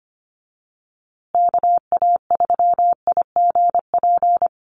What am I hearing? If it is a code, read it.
Morse code, heard as a keyed tone, about 25 wpm, 700 Hz. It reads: KA3IGP
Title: USA Call Signs Repeated 2x 25wpm